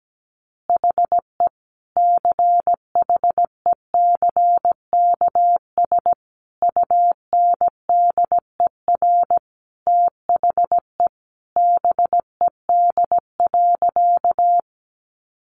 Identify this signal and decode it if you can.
Morse code, heard as a keyed tone, about 17 wpm, 700 Hz. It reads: HE CHECKS UNDER THE BED.
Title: Easy Phrases 17wpm